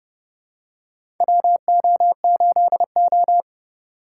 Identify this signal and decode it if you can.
Morse code, heard as a keyed tone, about 30 wpm, 700 Hz. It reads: WO8O
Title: USA Call Signs - 4 Characters - 3x Character Spacing - 30wpm